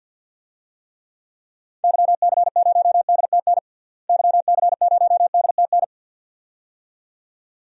Morse code, heard as a keyed tone, about 51 wpm, 700 Hz. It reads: YC0DTN YC0DTN
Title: DXCC-100 Call Signs - 6 Characters - Repeated 2x - 50wpm